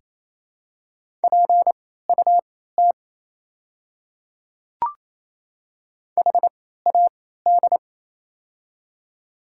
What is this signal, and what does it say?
Morse code, heard as a keyed tone, about 28 wpm, spaced as for 16 wpm, 690 Hz. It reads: PUT HAD
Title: 3x Character Spacing - Top 500 Words - 28wpm